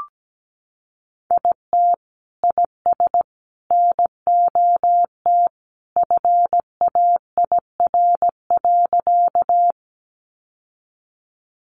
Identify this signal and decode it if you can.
Morse code, heard as a keyed tone, about 17 wpm, 700 Hz. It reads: IT IS NOT FAIR.
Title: Sentences from Top 1300 Words 17wpm